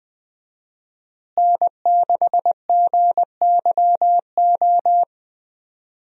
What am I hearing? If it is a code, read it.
Morse code, heard as a keyed tone, about 20 wpm, 700 Hz. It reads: N6GYO
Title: USA Call Signs - 5 Characters - Repeated 3x - 20wpm